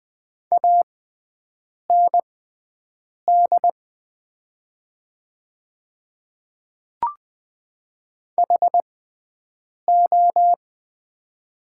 Morse code, heard as a keyed tone, about 20 wpm, spaced as for 7 wpm, 690 Hz. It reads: AND HO
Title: 6x Character Spacing - Top 100 Words - 20wpm